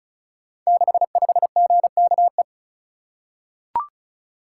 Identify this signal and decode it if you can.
Morse code, heard as a keyed tone, about 36 wpm, 700 Hz. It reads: 65GKE
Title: Code Groups - 5 Letter:Numbers - 35wpm